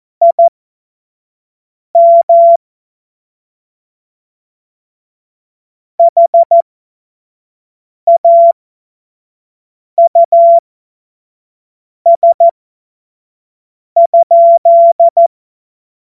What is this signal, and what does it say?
Morse code, heard as a keyed tone, about 14 wpm, spaced as for 5 wpm, 680 Hz. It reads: IM HAUS?
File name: Morsekurs / Witze / Witz 15. Juni 2021, 25 F70